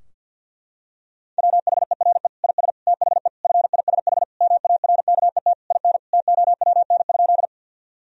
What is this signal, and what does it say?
Morse code, heard as a keyed tone, about 47 wpm, 700 Hz. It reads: WHERE IS THE FISH MARKET IN TOWN?
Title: Sentences from Top 1000 Words - Review Part 3 - 50wpm